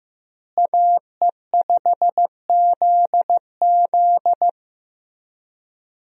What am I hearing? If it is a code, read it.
Morse code, heard as a keyed tone, about 15 wpm, 700 Hz. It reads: AE5ZZ
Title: USA Call Signs - 5 Characters - Repeated 2x - 15wpm